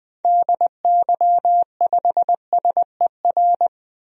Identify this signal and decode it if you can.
Morse code, heard as a keyed tone, about 20 wpm, 700 Hz. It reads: DY5SER
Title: DXCC-100 Call Signs - 6 Characters - 20wpm